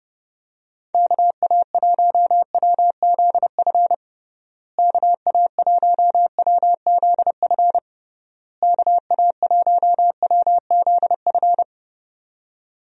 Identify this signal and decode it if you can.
Morse code, heard as a keyed tone, about 28 wpm, 700 Hz. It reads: KA1WZF KA1WZF KA1WZF
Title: USA Call Signs - 6 Characters - Repeated 3x - 30wpm